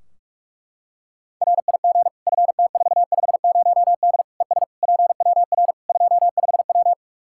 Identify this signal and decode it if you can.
Morse code, heard as a keyed tone, about 43 wpm, 700 Hz. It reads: RIG FT450D ES PWR 25W
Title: K1USN SST QSO Practice 45wpm